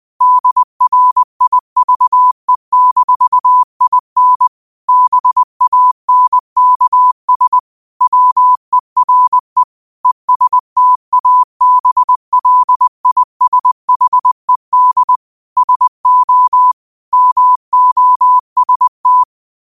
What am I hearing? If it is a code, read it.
Morse code, heard as a keyed tone, about 20 wpm, 1000 Hz. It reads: DRIVE-IN BANKS WERE ESTABLISHED SO MOST